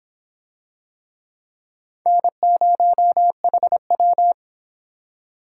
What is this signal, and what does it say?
Morse code, heard as a keyed tone, about 26 wpm, 700 Hz. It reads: N0HW